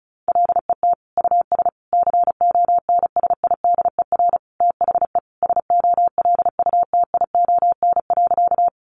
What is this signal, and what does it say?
Morse code, heard as a keyed tone, about 34 wpm, 700 Hz. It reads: LET US CONSIDER THE SOLUTION.
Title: Sentences from Top 1300 Words 35wpm